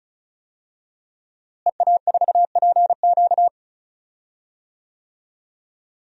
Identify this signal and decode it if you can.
Morse code, heard as a keyed tone, about 34 wpm, 700 Hz. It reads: EA4PQ
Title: DXCC-100 Call Signs - 5 Characters - Repeated 2x - 35wpm